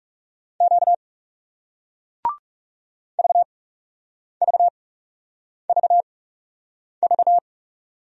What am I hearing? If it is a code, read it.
Morse code, heard as a keyed tone, about 39 wpm, spaced as for 8 wpm, 700 Hz. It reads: Q VVVV